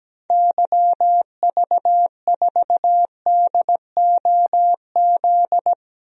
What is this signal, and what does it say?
Morse code, heard as a keyed tone, about 17 wpm, 700 Hz. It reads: YV4DOZ
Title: DXCC-100 Call Signs - 6 Characters - 17wpm